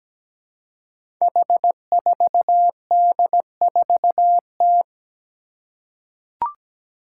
Morse code, heard as a keyed tone, about 17 wpm, 700 Hz. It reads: H4D4T
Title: Five Challenging Letter-Number Combinations 17wpm